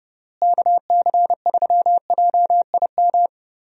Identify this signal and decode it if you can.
Morse code, heard as a keyed tone, about 32 wpm, 700 Hz. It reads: KC3JIM
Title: USA Call Signs - 6 Characters - Repeated 3x - 30wpm